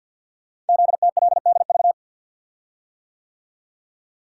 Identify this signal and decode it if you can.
Morse code, heard as a keyed tone, about 49 wpm, 700 Hz. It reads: ZTLDV
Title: Code Groups - 5 Letters - 50wpm